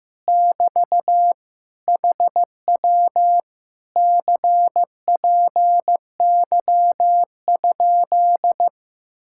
Morse code, heard as a keyed tone, about 15 wpm, 700 Hz. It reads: <BT> HW CPY?
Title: Speed-Racing Common QSO Phrases Encore - 15wpm